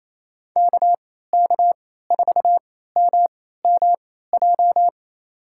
Morse code, heard as a keyed tone, about 28 wpm, spaced as for 16 wpm, 700 Hz. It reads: KK4MMJ
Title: USA Call Signs - 3x Character Spacing - 28wpm